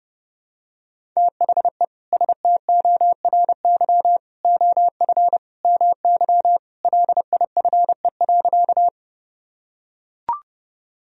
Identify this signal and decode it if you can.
Morse code, heard as a keyed tone, about 30 wpm, 700 Hz. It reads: THE STORY OF MY LIFE.
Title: Sentences from Top 1000 Words - Review Part 1 - 30wpm